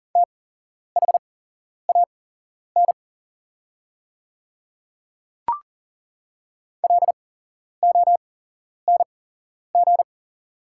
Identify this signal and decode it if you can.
Morse code, heard as a keyed tone, about 40 wpm, spaced as for 11 wpm, 690 Hz. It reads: THAN LONG
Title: 8x Character Spacing - Top 200 Words - 40wpm